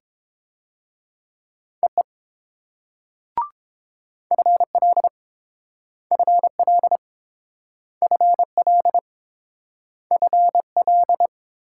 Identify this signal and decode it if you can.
Morse code, heard as a keyed tone, about 30 wpm, 700 Hz. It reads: EE FL FL FL FL